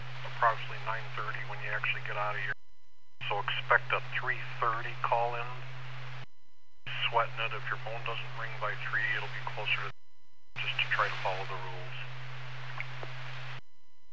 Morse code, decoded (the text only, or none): none